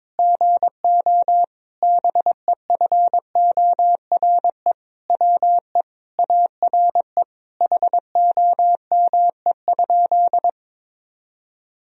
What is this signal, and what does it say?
Morse code, heard as a keyed tone, about 22 wpm, 700 Hz. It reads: GO BEFORE WE ARE HOME?
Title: Sentences from Top 1000 Words - Review Part 1 - 22wpm